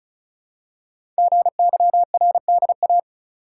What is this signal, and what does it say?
Morse code, heard as a keyed tone, about 36 wpm, 700 Hz. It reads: GYRDA
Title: Code Groups - 5 Letters - 35wpm